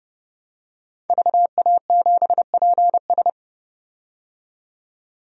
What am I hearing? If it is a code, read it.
Morse code, heard as a keyed tone, about 31 wpm, 700 Hz. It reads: VA7PS
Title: DXCC-100 Call Signs - 5 Characters - 30wpm